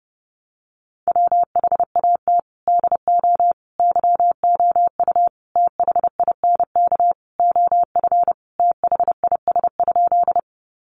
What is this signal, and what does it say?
Morse code, heard as a keyed tone, about 30 wpm, 700 Hz. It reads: WHAT DO YOU THINK OF THIS?
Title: Sentences from Top 200 Words 30wpm